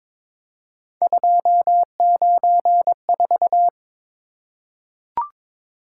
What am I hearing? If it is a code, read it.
Morse code, heard as a keyed tone, about 22 wpm, 700 Hz. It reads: 294
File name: practice / 22wpm / Three Letter-Number Combinations - 22wpm